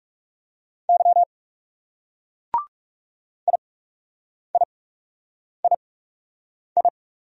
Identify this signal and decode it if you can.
Morse code, heard as a keyed tone, about 39 wpm, spaced as for 8 wpm, 700 Hz. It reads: Y IIII